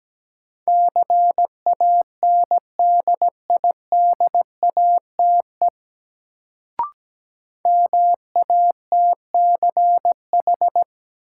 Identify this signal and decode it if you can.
Morse code, heard as a keyed tone, about 17 wpm, 700 Hz. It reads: CANDIDATE MATCH
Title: Sentences from Top 1600 Words - New Word Drill - 17wpm